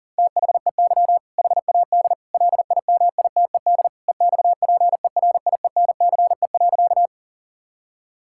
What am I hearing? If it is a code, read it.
Morse code, heard as a keyed tone, about 40 wpm, 700 Hz. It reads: THEY HAD LIMITED EXPERIENCE.
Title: Sentences from Top 1000 Words - Review Part 6 - 40wpm